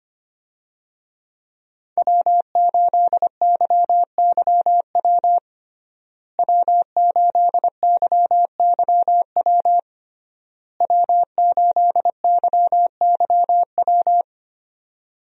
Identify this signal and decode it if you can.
Morse code, heard as a keyed tone, about 25 wpm, 700 Hz. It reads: W8YYW W8YYW W8YYW
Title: USA Call Signs - 5 Characters - Repeated 3x - 25wpm